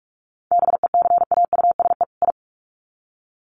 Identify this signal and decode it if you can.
Morse code, heard as a keyed tone, about 44 wpm, 700 Hz. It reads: BECAUSE I
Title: Sets of 2 Words 45wpm